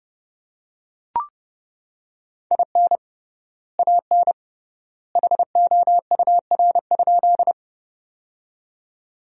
Morse code, heard as a keyed tone, about 30 wpm, 700 Hz. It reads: IN AN HOUR?
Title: 3x Word-Spacing - Sentences from Top 400 Words - 30wpm